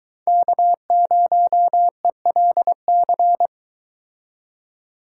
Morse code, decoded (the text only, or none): K0ELC